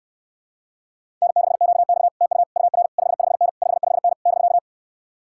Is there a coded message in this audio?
Yes: Morse code from 1.2 to 4.6 s, heard as a keyed tone, about 69 wpm, 700 Hz. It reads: N6QX TU UR 55N 55N <BK>